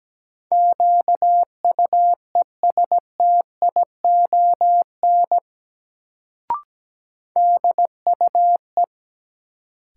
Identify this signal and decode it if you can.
Morse code, heard as a keyed tone, about 17 wpm, 700 Hz. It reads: QUESTION DUE
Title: Sets of 5 Words Encore - New Word Drill - 17wpm